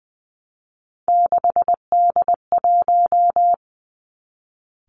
Morse code, heard as a keyed tone, about 20 wpm, 700 Hz. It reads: 6D1
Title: Three Challenging Letter-Number Combinations 20wpm